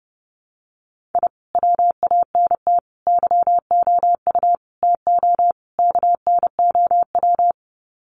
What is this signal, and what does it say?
Morse code, heard as a keyed tone, about 30 wpm, 700 Hz. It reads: I WANT YOU TO KNOW
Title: Sets of 5 Words Encore 30wpm